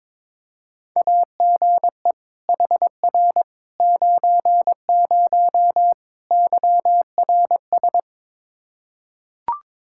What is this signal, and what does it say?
Morse code, heard as a keyed tone, about 21 wpm, 700 Hz. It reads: AGE HR 90 YRS
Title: Practice QSO - Exchange 3 - More Info - 22wpm